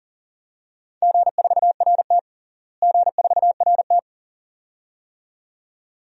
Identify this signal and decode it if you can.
Morse code, heard as a keyed tone, about 40 wpm, 700 Hz. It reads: G4RT G4RT
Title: DXCC-100 Call Signs - 4 Characters - Repeated 2x - 40wpm